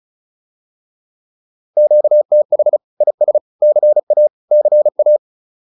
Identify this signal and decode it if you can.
Morse code, heard as a keyed tone, about 36 wpm, 600 Hz. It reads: QTH IS CA CA
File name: QSO102 35